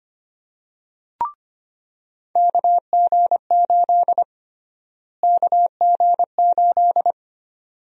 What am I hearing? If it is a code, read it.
Morse code, heard as a keyed tone, about 25 wpm, 700 Hz. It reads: KG8 KG8
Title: USA Call Sign Prefixes - Repeated 2x - 25wpm